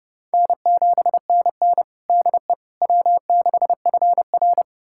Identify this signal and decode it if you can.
Morse code, heard as a keyed tone, about 30 wpm, 700 Hz. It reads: N7NN DE W6FR
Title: USA Callsign DE Callsign - 4 Characters - 30wpm